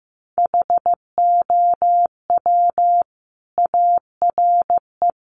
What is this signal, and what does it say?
Morse code, heard as a keyed tone, about 15 wpm, 700 Hz. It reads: HOW ARE
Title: Sentences from Top 1100 Words 15wpm